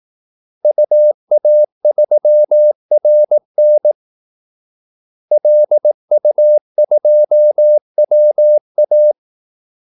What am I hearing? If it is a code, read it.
Morse code, heard as a keyed tone, about 18 wpm, 600 Hz. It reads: UA3RN LU2WA
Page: Kaldesignaler 18 wpm